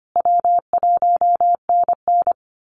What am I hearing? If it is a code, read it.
Morse code, heard as a keyed tone, about 25 wpm, 700 Hz. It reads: W1NN
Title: Easy Call Signs Repeated 3x - 25wpm